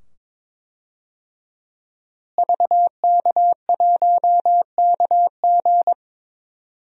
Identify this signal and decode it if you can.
Morse code, heard as a keyed tone, about 22 wpm, 700 Hz. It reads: VK1KG
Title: DXCC-100 Call Signs - 5 Characters - Repeated 2x - 22wpm